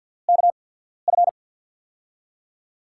Morse code, heard as a keyed tone, about 48 wpm, spaced as for 13 wpm, 700 Hz. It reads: KF
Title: USA Call Signs - 6 Characters - 8x Character Spacing - 50wpm